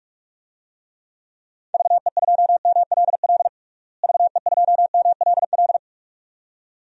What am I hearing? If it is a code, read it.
Morse code, heard as a keyed tone, about 46 wpm, 700 Hz. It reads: VE2MLL VE2MLL
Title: DXCC-100 Call Signs - 6 Characters - Repeated 2x - 45wpm